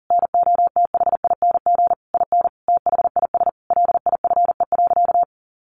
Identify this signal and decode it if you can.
Morse code, heard as a keyed tone, about 40 wpm, 700 Hz. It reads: NOTHING IN THIS LIFE.
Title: Sentences from Top 1200 Words 40wpm